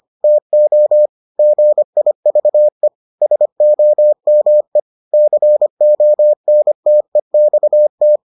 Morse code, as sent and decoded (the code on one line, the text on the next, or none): - --- / --. .. ...- . / ... --- -- . / -.-. --- -. - . -..- -
TO GIVE SOME CONTEXT